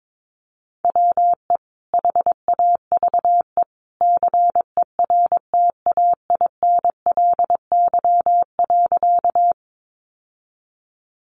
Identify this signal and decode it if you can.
Morse code, heard as a keyed tone, about 22 wpm, 700 Hz. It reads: WE HAVE CERTAINLY.
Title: Sentences from Top 900 Words 22wpm